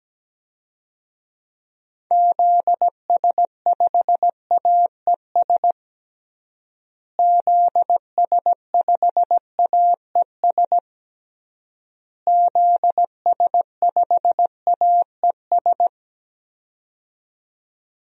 Morse code, heard as a keyed tone, about 17 wpm, 700 Hz. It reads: ZS5AES ZS5AES ZS5AES
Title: DXCC-100 Call Signs - 6 Characters - Repeated 3x - 17wpm